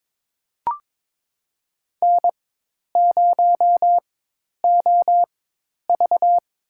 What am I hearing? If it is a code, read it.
Morse code, heard as a keyed tone, about 22 wpm, spaced as for 10 wpm, 700 Hz. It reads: N0OV
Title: USA Call Signs - 4x Character Spacing - 22wpm